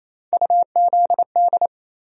Morse code, heard as a keyed tone, about 28 wpm, 700 Hz. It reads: UZD
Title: USA Call Sign Postfixes - 3 Letters 28wpm